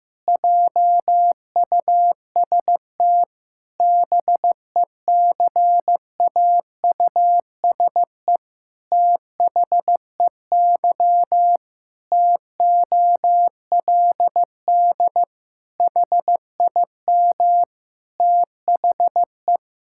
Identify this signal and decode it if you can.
Morse code, heard as a keyed tone, about 15 wpm, 700 Hz. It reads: JUST BECAUSE THEY TOLD HIM THE
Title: Pangram Challenge v2 - 15wpm